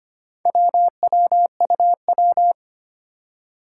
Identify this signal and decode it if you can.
Morse code, heard as a keyed tone, about 25 wpm, 700 Hz. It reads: WWUW
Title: Four Challenging Letter-Number Combinations 25wpm